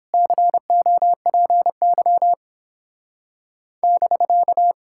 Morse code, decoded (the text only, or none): COPY <BK>